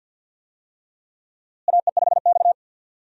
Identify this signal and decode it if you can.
Morse code, heard as a keyed tone, about 52 wpm, 700 Hz. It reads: AE5X
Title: USA Call Signs - 4 Characters - Repeated 3x - 50wpm